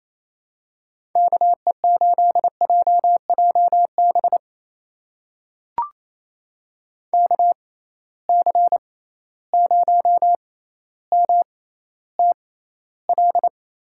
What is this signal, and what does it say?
Morse code, heard as a keyed tone, about 28 wpm, spaced as for 14 wpm, 700 Hz. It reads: KE8JJB KC0MTL
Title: USA Call Signs - 6 Characters - 6x Character Spacing - 28wpm